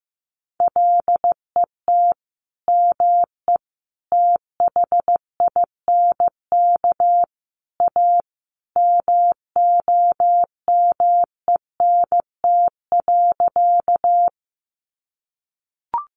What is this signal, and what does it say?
Morse code, heard as a keyed tone, about 14 wpm, 700 Hz. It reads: LET ME THINK A MOMENT.
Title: Sentences from Top 1000 Words 15wpm